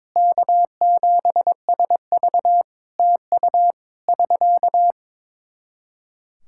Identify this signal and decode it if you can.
Morse code, heard as a keyed tone, about 23 wpm, 700 Hz. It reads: K7SV TU <SK>
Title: Practice QSO - Exchange 4 - Ending - 22wpm